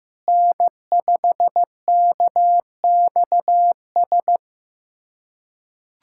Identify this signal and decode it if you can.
Morse code, heard as a keyed tone, about 15 wpm, 700 Hz. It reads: N5KXS
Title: USA Callsign DE Callsign - 5 Characters - 15wpm